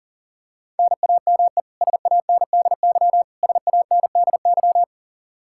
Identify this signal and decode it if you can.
Morse code, heard as a keyed tone, about 40 wpm, 700 Hz. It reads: NAME SANDY SANDY
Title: K1USN SST QSO Practice 40wpm